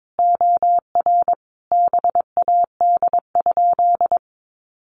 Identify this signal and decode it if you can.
Morse code, heard as a keyed tone, about 22 wpm, 700 Hz. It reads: OR BAD?
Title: Sentences from Top 800 Words 22wpm